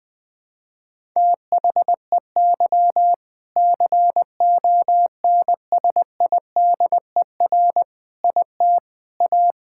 Morse code, heard as a keyed tone, about 20 wpm, 700 Hz. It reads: THEY CONSIDER IT A
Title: Sentences from Top 1000 Words - Review Part 6 - 20wpm